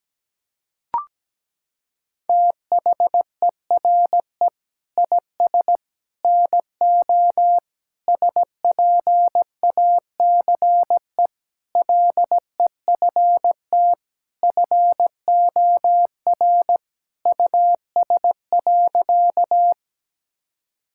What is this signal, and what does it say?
Morse code, heard as a keyed tone, about 17 wpm, 700 Hz. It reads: THERE IS NO SPACE LEFT FOR US.
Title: Sentences from Top 1000 Words - Review Part 5 - 17wpm